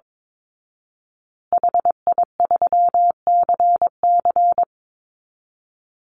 Morse code, heard as a keyed tone, about 22 wpm, 700 Hz. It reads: HI3CC
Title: Call Signs Repeated 3x - 22wpm